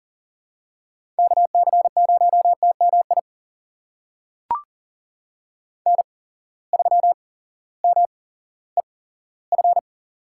Morse code, heard as a keyed tone, about 40 wpm, spaced as for 18 wpm, 700 Hz. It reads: KC0TMI N3MEF
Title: USA Call Signs - 8x Character Spacing - 40wpm